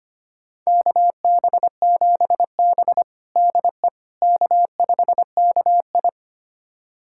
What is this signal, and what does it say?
Morse code, heard as a keyed tone, about 25 wpm, 700 Hz. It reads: KB7B DE K5KI
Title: USA Callsign DE Callsign - 4 Characters - 25wpm